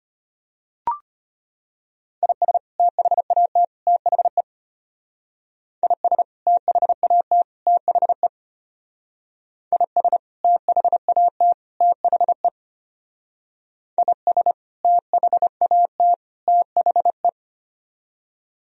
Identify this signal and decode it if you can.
Morse code, heard as a keyed tone, about 30 wpm, 700 Hz. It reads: IS THAT THE IS THAT THE IS THAT THE IS THAT THE
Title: Speed-Racing - Sets of 3 Words 25wpm